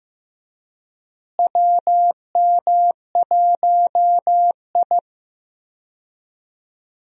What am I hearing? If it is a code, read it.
Morse code, heard as a keyed tone, about 15 wpm, 700 Hz. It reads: WM1I
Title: USA Call Signs - 4 Characters - Repeated 3x - 15wpm